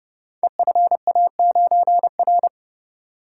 Morse code, heard as a keyed tone, about 30 wpm, 700 Hz. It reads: EFA9R
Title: Code Groups - 5 Letter:Numbers - 30wpm